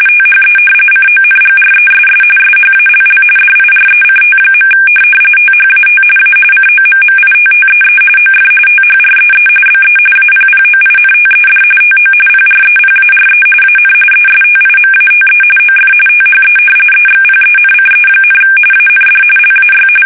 RTTY, about 45.45 baud. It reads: THE CONGRESS SHALL DIRECT. THAT A COMMITTEE OF THREE BE APPOINTED TO PREPARE AN ESTIMATE OF THE EXPENCE, AND LAY
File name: fsk850 / usnavy1